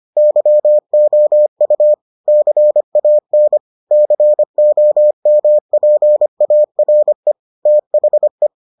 Morse code, heard as a keyed tone, about 25 wpm, 600 Hz. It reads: YOU CAN COMPARE THE